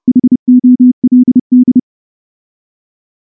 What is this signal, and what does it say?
Morse code, heard as a keyed tone, about 29 wpm, 250 Hz. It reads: HOLD
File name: morse / morse-training-30WPM